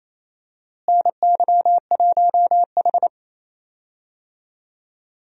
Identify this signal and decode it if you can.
Morse code, heard as a keyed tone, about 28 wpm, 700 Hz. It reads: NY1H
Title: USA Call Signs - 4 Characters - Repeated 2x - 28wpm